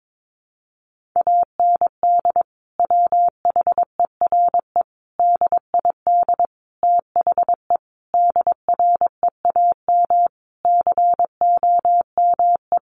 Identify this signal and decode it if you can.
Morse code, heard as a keyed tone, about 22 wpm, 700 Hz. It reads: AND WHERE DID THE DREAM COME
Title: Sentences from Top 1500 Words 22wpm